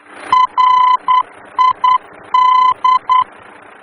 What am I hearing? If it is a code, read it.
Morse code, heard as a keyed tone, about 10 wpm, 1010 Hz. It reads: RID
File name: germany / mx / RID-D-VOR